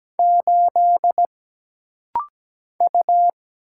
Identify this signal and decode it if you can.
Morse code, heard as a keyed tone, about 17 wpm, 700 Hz. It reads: 8 U
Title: Single Letter-Number - Rapid-Fire 17wpm